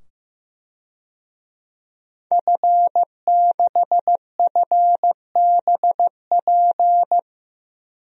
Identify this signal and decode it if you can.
Morse code, heard as a keyed tone, about 15 wpm, 700 Hz. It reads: F6FBP
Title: DXCC-100 Call Signs - 5 Characters - 15wpm